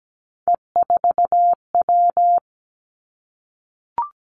Morse code, heard as a keyed tone, about 17 wpm, 700 Hz. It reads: E4W
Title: Three Challenging Letter-Number Combinations 17wpm